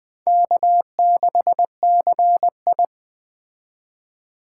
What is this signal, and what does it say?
Morse code, heard as a keyed tone, about 20 wpm, 700 Hz. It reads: K6CI
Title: Speed Racing - USA 1x2 Call Signs 17wpm